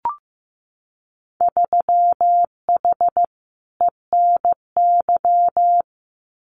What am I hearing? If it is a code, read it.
Morse code, heard as a keyed tone, about 15 wpm, 700 Hz. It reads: 3H ENY
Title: Winter Field Day Exchanges Encore 15wpm